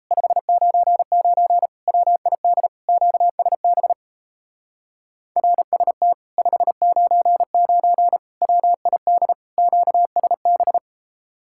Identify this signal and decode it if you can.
Morse code, heard as a keyed tone, about 38 wpm, 700 Hz. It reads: 599 WID QSB RST 599 WID QSB